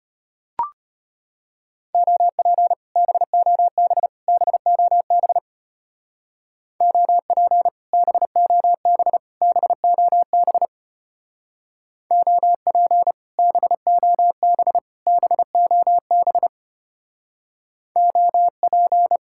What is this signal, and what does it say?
Morse code, heard as a keyed tone, about 30 wpm, 700 Hz. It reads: OP BOB BOB OP BOB BOB OP BOB BOB OP
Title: Speed-Racing Common QSO Phrases Encore - 25wpm